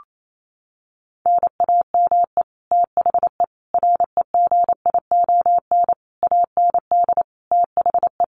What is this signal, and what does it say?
Morse code, heard as a keyed tone, about 28 wpm, 700 Hz. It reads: NAME THE REGION AND THE
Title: Sentences from Top 1500 Words 28wpm